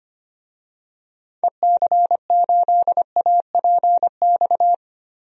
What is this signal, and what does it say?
Morse code, heard as a keyed tone, about 25 wpm, 700 Hz. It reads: EC8APX
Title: DXCC-100 Call Signs - 6 Characters - Repeated 3x - 25wpm